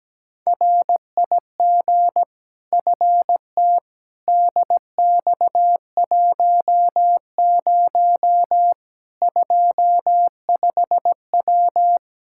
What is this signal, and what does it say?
Morse code, heard as a keyed tone, about 17 wpm, 700 Hz. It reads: RIG FT DX10 25W
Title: Practice QSO - Exchange 2 - Info - 17wpm